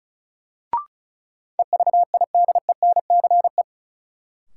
Morse code, heard as a keyed tone, about 36 wpm, 700 Hz. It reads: EVIDENCE
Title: Sets of 5 Words Encore - New Word Drill - 35wpm